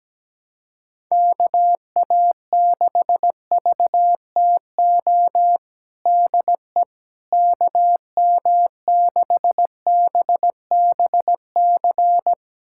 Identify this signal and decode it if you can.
Morse code, heard as a keyed tone, about 17 wpm, 700 Hz. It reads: KA6VTO DE KM6BBC
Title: USA Callsign DE Callsign 17wpm